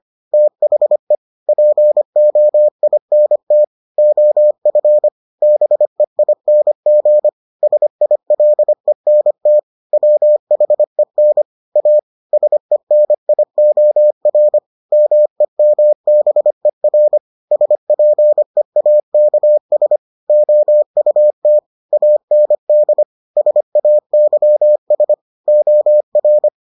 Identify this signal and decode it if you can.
Morse code, heard as a keyed tone, about 25 wpm, 600 Hz. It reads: THE POINT OF BEING SILENT WHEN A SENIOR MEMBER SPEAKS OUT AND SAYS OR